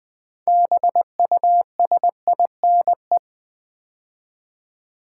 Morse code, heard as a keyed tone, about 20 wpm, 700 Hz. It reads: BUSINE
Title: Build A Word - 3 Letters At A Time - 8-Letter Words - 20wpm